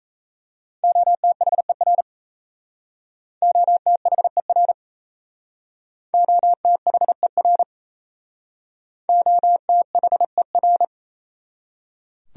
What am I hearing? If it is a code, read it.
Morse code, heard as a keyed tone, about 34 wpm, 700 Hz. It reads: OTHER OTHER OTHER OTHER
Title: Speed-Racing Top 300 Words 28wpm